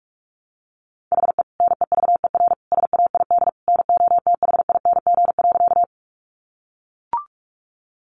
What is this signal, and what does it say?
Morse code, heard as a keyed tone, about 44 wpm, 700 Hz. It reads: HE NEVER SAID NOTHING.
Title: Sentences from Top 500 Words 45wpm